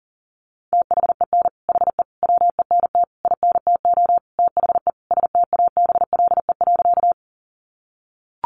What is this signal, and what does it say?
Morse code, heard as a keyed tone, about 40 wpm, 700 Hz. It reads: THEN HE WENT INTO THE STABLE.
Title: Sentences from Top 1500 Words 40wpm